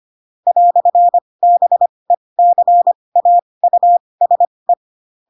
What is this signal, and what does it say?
Morse code, heard as a keyed tone, about 25 wpm, spaced as for 20 wpm, 700 Hz. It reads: "BECAUSE
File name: Pinocchio / Pinocchio ch7 15 25 ik1hge